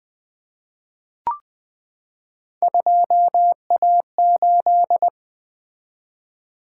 Morse code, heard as a keyed tone, about 20 wpm, 700 Hz. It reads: 2A8
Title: Three Letter-Number Combinations - 20wpm